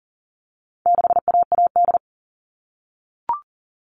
Morse code, heard as a keyed tone, about 40 wpm, 700 Hz. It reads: 6AAD
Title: All Cut Numbers - 4-Digit 40wpm